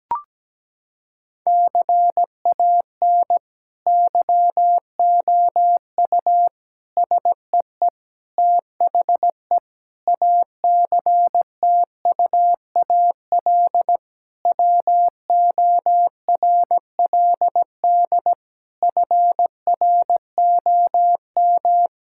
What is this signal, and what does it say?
Morse code, heard as a keyed tone, about 17 wpm, 700 Hz. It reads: CAN YOU SEE THE ACTUAL WORLD FROM
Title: Sentences from Top 1000 Words - Review Part 5 - 17wpm